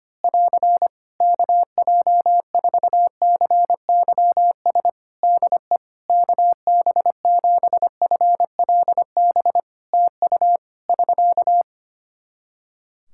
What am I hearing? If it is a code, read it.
Morse code, heard as a keyed tone, about 25 wpm, 700 Hz. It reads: <AR> KJ4CYS DE KB7FLB TU <SK>
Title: Practice QSO - Exchange 4 - Ending - 25wpm